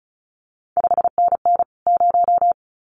Morse code, heard as a keyed tone, about 36 wpm, 700 Hz. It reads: 5NN 0
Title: CQ WW DX Contest Exchange Encore - 35wpm